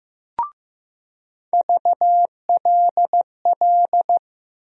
Morse code, heard as a keyed tone, about 15 wpm, 700 Hz. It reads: VLL
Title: USA Call Sign Postfixes - 3 Letters 15wpm